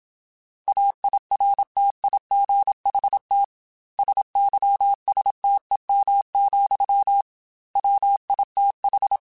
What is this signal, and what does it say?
Morse code, heard as a keyed tone, about 26 wpm, 800 Hz. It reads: AIRTIGHT SYSTEM, WITH
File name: Quotes / Quotes-mp3s / Airtight @25